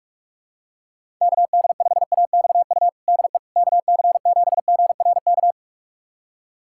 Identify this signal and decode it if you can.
Morse code, heard as a keyed tone, about 43 wpm, 700 Hz. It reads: KD5AXU DE KC7GRK
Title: USA Callsign DE Callsign - 6 Characters - 45wpm